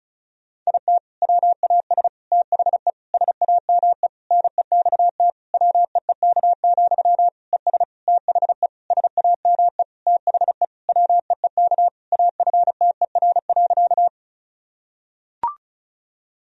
Morse code, heard as a keyed tone, about 36 wpm, 700 Hz. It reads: IT WAS THE SAME NEXT WEEK, ES THE SAME THE WEEK AFTER.
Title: Sentences from Top 500 Words and Abbreviations 35wpm